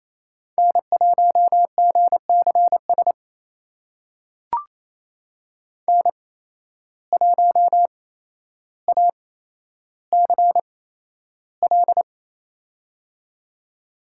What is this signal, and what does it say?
Morse code, heard as a keyed tone, about 28 wpm, spaced as for 12 wpm, 700 Hz. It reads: N1GCS N1ACL
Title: USA Call Signs - 5 Characters - 8x Character Spacing - 28wpm